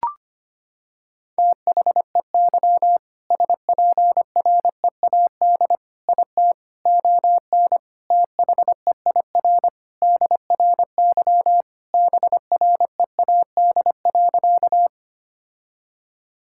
Morse code, heard as a keyed tone, about 25 wpm, 700 Hz. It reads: THEY SPREAD IT ON THEIR DRY BREAD.
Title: Sentences from Top 1600 Words 25wpm